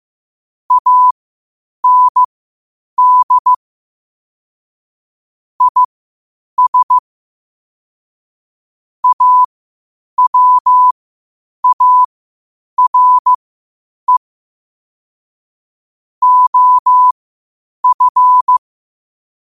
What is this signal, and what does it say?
Morse code, heard as a keyed tone, about 15 wpm, spaced as for 8 wpm, 1000 Hz. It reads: AND IS AWARE OF
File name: qotdfiles / 8wpm